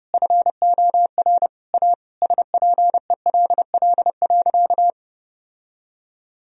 Morse code, heard as a keyed tone, about 30 wpm, 700 Hz. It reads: FOR A SPELL.
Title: Sentences from Top 1000 Words - Review Part 1 - 30wpm